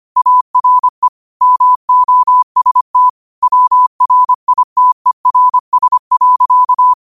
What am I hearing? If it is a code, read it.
Morse code, heard as a keyed tone, about 25 wpm, 1000 Hz. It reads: ARE MOST WRITERS.
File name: qotdfiles / 25wpm